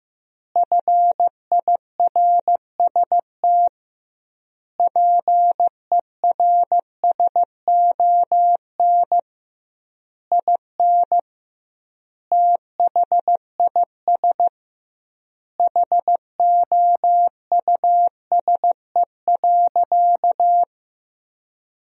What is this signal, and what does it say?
Morse code, heard as a keyed tone, about 15 wpm, 700 Hz. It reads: FIRST PERSON IN THIS HOUSE.
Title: 2x Word-Spacing - Sentences from Top 400 Words - 15wpm